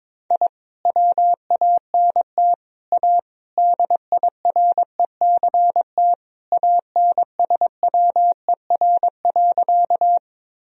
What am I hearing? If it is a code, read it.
Morse code, heard as a keyed tone, about 22 wpm, 700 Hz. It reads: I WANT A DIRECT ANSWER.
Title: Sentences from Top 1000 Words - Review Part 2 - 22wpm